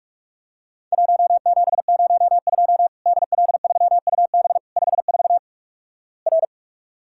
Morse code, heard as a keyed tone, about 44 wpm, 700 Hz. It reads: 1702 DL3UB 54 R
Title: Worked All Europe DX Contest - QTC Exchange - 45wpm